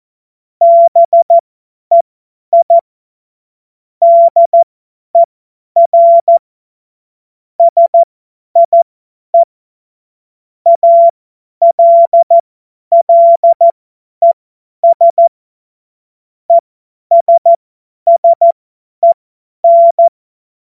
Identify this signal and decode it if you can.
Morse code, heard as a keyed tone, about 14 wpm, spaced as for 10 wpm, 680 Hz. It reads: BEI DER SIE ALLES ESSEN
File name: Morsekurs / Witze / Witz 19 März 2024 50 F70